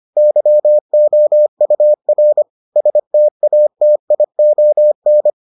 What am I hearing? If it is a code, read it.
Morse code, heard as a keyed tone, about 25 wpm, 600 Hz. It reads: YOUR STATION